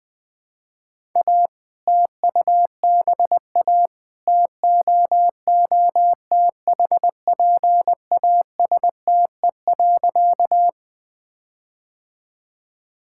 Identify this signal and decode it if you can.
Morse code, heard as a keyed tone, about 21 wpm, 700 Hz. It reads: A TUBA TOOTHPASTE.
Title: All-Age Appropriate Jokes - 20wpm